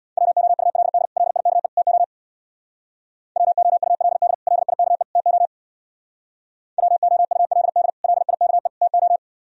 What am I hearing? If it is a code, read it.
Morse code, heard as a keyed tone, about 62 wpm, 700 Hz. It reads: WOULD LIKE TO WOULD LIKE TO WOULD LIKE TO
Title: Speed-Racing - Sets of 3 Words 50wpm